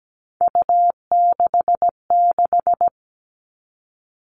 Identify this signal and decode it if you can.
Morse code, heard as a keyed tone, about 17 wpm, 700 Hz. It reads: U66
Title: All Cut Numbers - 3-Digit 17wpm